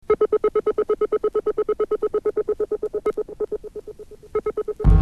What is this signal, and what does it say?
Congestion tone (425 Hz).